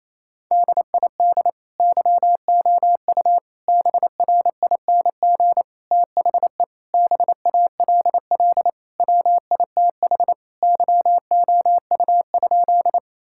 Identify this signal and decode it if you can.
Morse code, heard as a keyed tone, about 28 wpm, 700 Hz. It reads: DID YOU BRING THE BALL WITH YOU?
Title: Sentences from Top 1000 Words - Review Part 6 - 28wpm